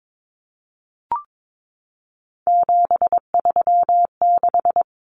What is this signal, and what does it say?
Morse code, heard as a keyed tone, about 22 wpm, 700 Hz. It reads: 736
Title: Common Cut Numbers - 3-Digit 22wpm